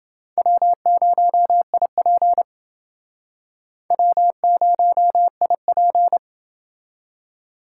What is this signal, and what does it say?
Morse code, heard as a keyed tone, about 29 wpm, 700 Hz. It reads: W0IP W0IP